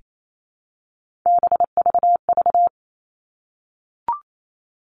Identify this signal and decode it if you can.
Morse code, heard as a keyed tone, about 28 wpm, 700 Hz. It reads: BVV
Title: All Cut Numbers - 3-Digit 28wpm